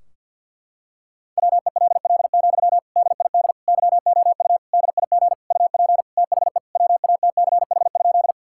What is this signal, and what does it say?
Morse code, heard as a keyed tone, about 51 wpm, 700 Hz. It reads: WELL, DID YOU DIG UP THE WATCH?
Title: Sentences from Top 1600 Words 50wpm